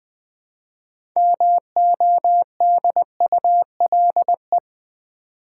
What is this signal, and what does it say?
Morse code, heard as a keyed tone, about 20 wpm, 700 Hz. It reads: MODULE
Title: Build A Word - 2 Letters At A Time - 6-Letter Words - 20wpm